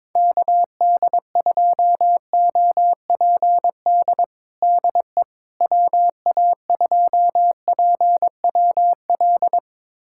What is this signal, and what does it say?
Morse code, heard as a keyed tone, about 22 wpm, 700 Hz. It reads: KD2OPD DE WA2PWL
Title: USA Callsign DE Callsign 22wpm